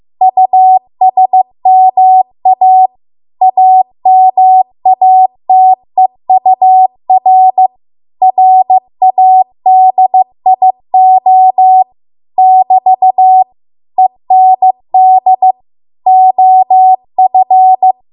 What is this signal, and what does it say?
Morse code, heard as a keyed tone, about 15 wpm, 750 Hz. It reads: USMA AMATEUR RADIO <BT> END OF